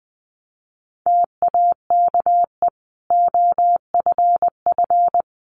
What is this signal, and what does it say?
Morse code, heard as a keyed tone, about 20 wpm, 700 Hz. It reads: TAKE OFF
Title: Sentences from Top 1400 Words 20wpm